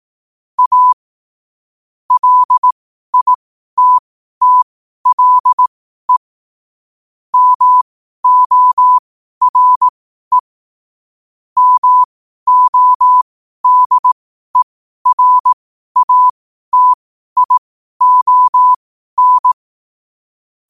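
Morse code, heard as a keyed tone, about 18 wpm, spaced as for 12 wpm, 1000 Hz. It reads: A LITTLE MORE MODERATION
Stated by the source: 12 wpm Morse.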